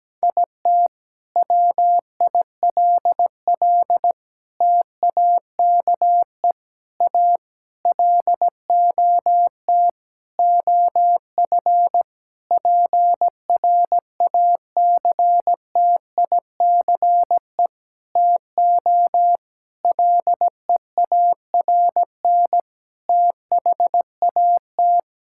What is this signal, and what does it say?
Morse code, heard as a keyed tone, about 17 wpm, 700 Hz. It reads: IT WILL TAKE A LOT OF PRACTICE TO LEARN THAT
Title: Sentences from Top 1000 Words - Review Part 6 - 17wpm